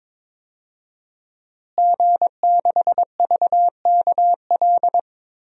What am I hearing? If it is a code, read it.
Morse code, heard as a keyed tone, about 22 wpm, 700 Hz. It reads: G6VKL
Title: DXCC-100 Call Signs - 5 Characters - Repeated 3x - 22wpm